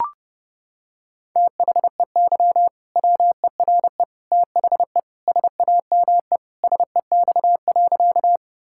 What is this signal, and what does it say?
Morse code, heard as a keyed tone, about 30 wpm, 700 Hz. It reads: THEY WERE THE SAME SEX.
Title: Sentences from Top 1600 Words 30wpm